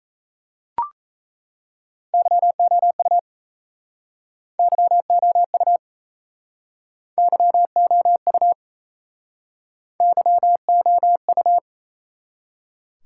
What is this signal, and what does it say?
Morse code, heard as a keyed tone, about 36 wpm, 700 Hz. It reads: YOU YOU YOU YOU